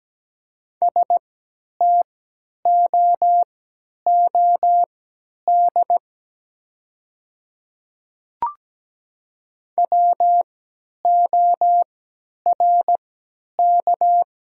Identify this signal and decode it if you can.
Morse code, heard as a keyed tone, about 17 wpm, spaced as for 10 wpm, 700 Hz. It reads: STOOD WORK
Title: 3x Character Spacing - Top 500 Words - 17wpm